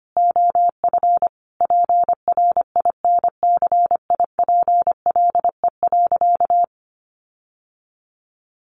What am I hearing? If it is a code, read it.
Morse code, heard as a keyed tone, about 25 wpm, 700 Hz. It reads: OF PRINCIPLE.
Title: Sentences from Top 1400 Words 25wpm